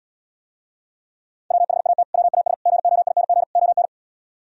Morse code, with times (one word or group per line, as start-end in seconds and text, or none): 1.5-2.0 WHAT
2.1-2.5 WAS
2.7-3.4 GOING
3.6-3.9 ON